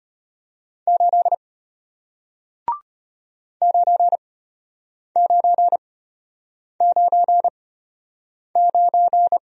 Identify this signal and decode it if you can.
Morse code, heard as a keyed tone, about 34 wpm, spaced as for 8 wpm, 700 Hz. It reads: 8 9999